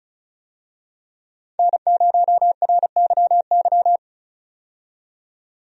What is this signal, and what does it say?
Morse code, heard as a keyed tone, about 36 wpm, 700 Hz. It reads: N0RYY
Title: USA Call Signs - 5 Characters - 35wpm